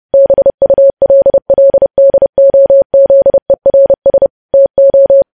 Morse code, heard as a keyed tone, about 30 wpm, 550 Hz. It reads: BULLDOZERS TO
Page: news headlines in morse code at 30 wpm